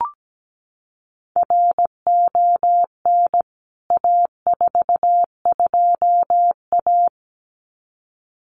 Morse code, heard as a keyed tone, about 17 wpm, 700 Hz. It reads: RON A42A
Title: CWT Exchange 17wpm